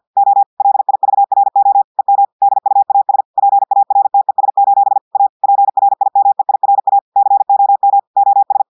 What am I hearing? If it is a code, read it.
Morse code, heard as a keyed tone, about 50 wpm, 800 Hz. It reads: O LIVRO EM DUAS PARTES: A PRIMEIRA COM OS
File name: audios / portugais / portugais texte en clair 1 vitesse 50 mots minutes